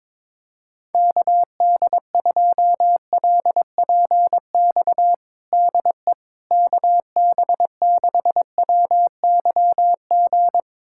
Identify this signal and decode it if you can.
Morse code, heard as a keyed tone, about 22 wpm, 700 Hz. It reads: KD2LPX DE KB6WYG
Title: USA Callsign DE Callsign - 6 Characters - 22wpm